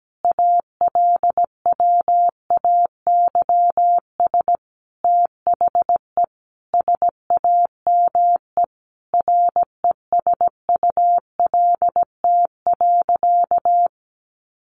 Morse code, heard as a keyed tone, about 17 wpm, 700 Hz. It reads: ALWAYS THE SAME RESULT.
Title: Sentences from Top 700 Words 17wpm